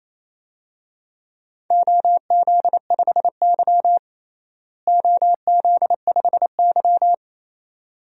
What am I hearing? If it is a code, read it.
Morse code, heard as a keyed tone, about 28 wpm, 700 Hz. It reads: OZ5Y OZ5Y
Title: DXCC-100 Call Signs - 4 Characters - Repeated 2x - 28wpm